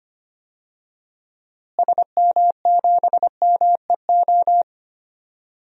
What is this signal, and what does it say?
Morse code, heard as a keyed tone, about 25 wpm, 700 Hz. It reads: SM7MEO
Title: DXCC-100 Call Signs - 6 Characters - Repeated 3x - 25wpm